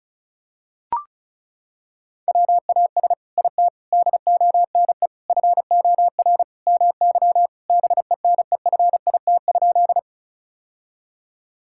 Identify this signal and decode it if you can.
Morse code, heard as a keyed tone, about 34 wpm, 700 Hz. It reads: WAS IT DONE FOR MY BENEFIT?
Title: Sentences from Top 1000 Words - Review Part 6 - 35wpm